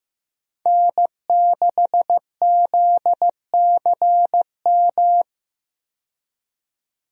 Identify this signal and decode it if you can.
Morse code, heard as a keyed tone, about 15 wpm, 700 Hz. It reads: N6ZCM
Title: USA Call Signs - 5 Characters - 15wpm